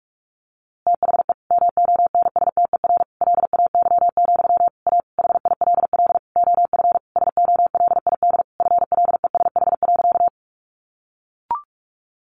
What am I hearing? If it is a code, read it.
Morse code, heard as a keyed tone, about 46 wpm, 700 Hz. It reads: THE MONSTER LAY, A HILL OF SOLID FLESH.
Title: A Sound Of Thunder 45wpm